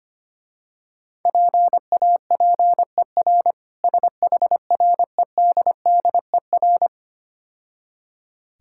Morse code, heard as a keyed tone, about 25 wpm, 700 Hz. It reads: PAPER SHREDDER
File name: Easy Words - Speed-Racing with Hints - 25wpm